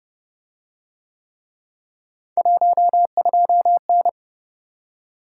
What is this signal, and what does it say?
Morse code, heard as a keyed tone, about 29 wpm, 700 Hz. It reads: 12N